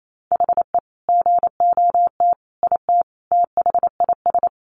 Morse code, heard as a keyed tone, about 28 wpm, 700 Hz. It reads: HE GOT IT THIS
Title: Sentences from Top 400 Words 28wpm